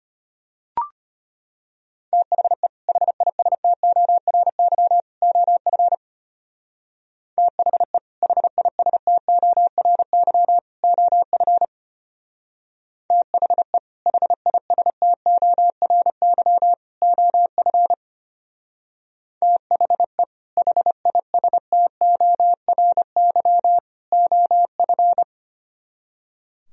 Morse code, heard as a keyed tone, about 32 wpm, 700 Hz. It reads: THE HISTORY OF THE HISTORY OF THE HISTORY OF THE HISTORY OF